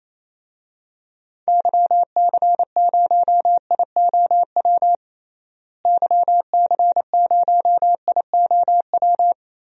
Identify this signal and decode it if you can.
Morse code, heard as a keyed tone, about 28 wpm, 700 Hz. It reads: YC0IOW YC0IOW
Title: DXCC-100 Call Signs - 6 Characters - Repeated 3x - 28wpm